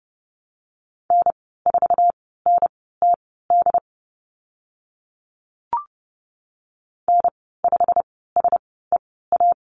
Morse code, heard as a keyed tone, about 30 wpm, spaced as for 17 wpm, 700 Hz. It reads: N4NTD N5SEA
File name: course / 30:10wpm / 088 Lesson 15g - Number 5 - Call Signs 30wpm